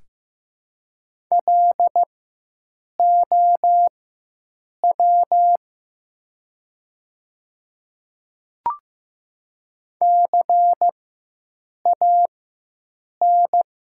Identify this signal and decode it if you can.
Morse code, heard as a keyed tone, about 15 wpm, spaced as for 7 wpm, 700 Hz. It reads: LOW CAN
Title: 4x Character Spacing - Top 300 Words - 15wpm